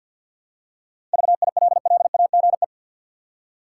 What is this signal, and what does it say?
Morse code, heard as a keyed tone, about 50 wpm, 700 Hz. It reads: VILLAGE
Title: Seven Letter Words - 50wpm